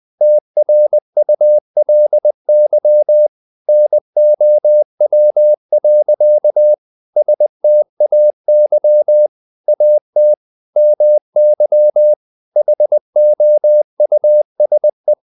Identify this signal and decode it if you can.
Morse code, heard as a keyed tone, about 20 wpm, 600 Hz. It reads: TRULY NOW. STAY AT MY HOUSE
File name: the old man and the sea / the old man and the sea part 1 20wpm